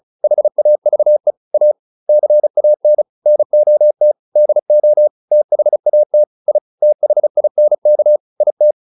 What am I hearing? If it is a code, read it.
Morse code, heard as a keyed tone, about 34 wpm, 600 Hz. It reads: HAVE A CAN NOT DO THAT I THINK IT
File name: WD403 35